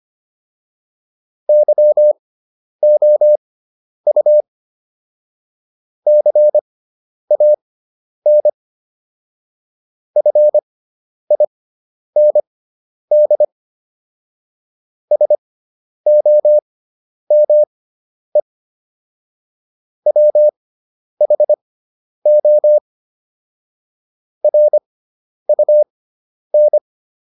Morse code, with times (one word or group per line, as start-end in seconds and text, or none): 1.5-4.4 YOU
6.1-8.5 CAN
10.2-13.4 FIND
15.1-18.4 SOME
20.1-22.8 WHO
24.4-26.8 RUN